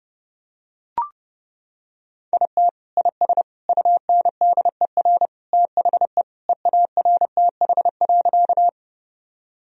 Morse code, heard as a keyed tone, about 30 wpm, 700 Hz. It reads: IT IS UNDER THE EARTH.
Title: Sentences from Top 1000 Words - Review Part 1 - 30wpm